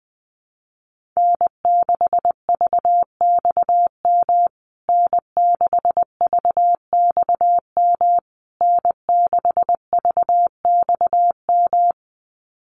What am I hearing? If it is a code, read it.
Morse code, heard as a keyed tone, about 20 wpm, 700 Hz. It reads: N6VXM N6VXM N6VXM
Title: USA Call Signs Repeated 3x 20wpm